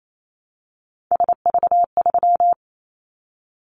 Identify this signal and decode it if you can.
Morse code, heard as a keyed tone, about 28 wpm, 700 Hz. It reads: SV3
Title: Callsign Prefixes for Top 50 DXCC - 28wpm